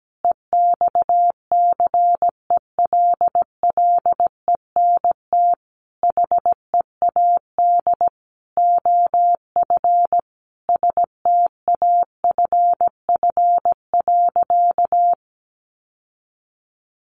Morse code, heard as a keyed tone, about 17 wpm, 700 Hz. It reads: EXCELLENT HEAD OF STAFF.
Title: Sentences from Top 1200 Words 17wpm